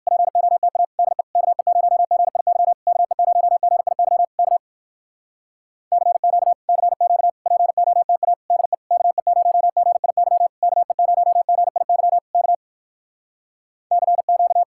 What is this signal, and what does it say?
Morse code, heard as a keyed tone, about 67 wpm, 700 Hz. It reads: POTA DE KE0GIY KE0GIY K CQ CQ POTA DE KE0GIY KE0GIY K CQ